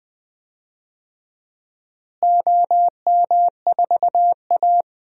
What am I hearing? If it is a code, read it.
Morse code, heard as a keyed tone, about 20 wpm, 700 Hz. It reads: OM4A
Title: DXCC-100 Call Signs - 4 Characters - Repeated 2x - 20wpm